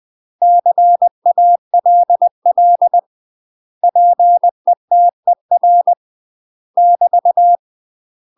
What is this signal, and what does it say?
Morse code, heard as a keyed tone, about 20 wpm, 700 Hz. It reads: CALL PETER <BT>
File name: UK MASTER QSOs TRAINING / 20WPM / UK MASTER QSOs 0042